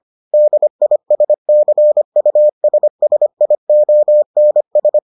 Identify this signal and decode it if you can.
Morse code, heard as a keyed tone, about 25 wpm, 600 Hz. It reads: DISCUSSIONS